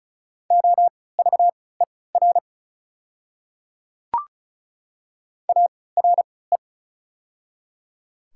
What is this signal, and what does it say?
Morse code, heard as a keyed tone, about 36 wpm, spaced as for 20 wpm, 700 Hz. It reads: OVER ARE
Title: 3x Character Spacing - Top 100 Words - 35wpm